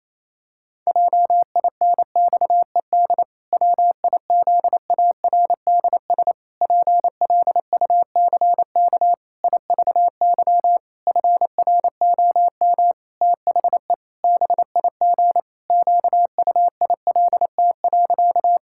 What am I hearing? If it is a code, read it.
Morse code, heard as a keyed tone, about 28 wpm, 700 Hz. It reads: JINXED WIZARDS PLUCK IVY FROM THE BIG QUILT.
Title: Pangram Challenge v2 - 28wpm